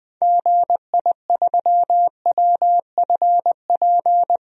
Morse code, heard as a keyed tone, about 20 wpm, 700 Hz. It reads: GI3WFP
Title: DXCC-100 Call Signs - 6 Characters - 20wpm